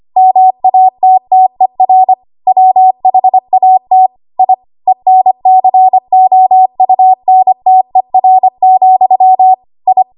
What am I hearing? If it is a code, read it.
Morse code, heard as a keyed tone, about 25 wpm, 750 Hz. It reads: MATTER WHAT I ENCOUNTER, I
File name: Morse / 25WPM